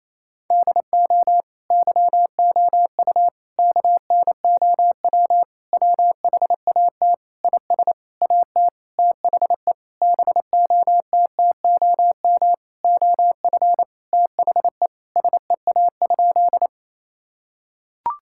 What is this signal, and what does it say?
Morse code, heard as a keyed tone, about 28 wpm, 700 Hz. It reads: DO YOU KNOW WHAT IS AT THE BOTTOM OF THE SEA?
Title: Sentences from Top 1000 Words - Review Part 6 - 28wpm